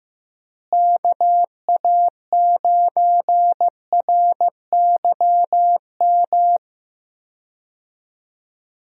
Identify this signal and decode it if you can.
Morse code, heard as a keyed tone, about 16 wpm, 700 Hz. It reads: KA9RYM
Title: USA Call Signs - 6 Characters - Repeated 3x - 15wpm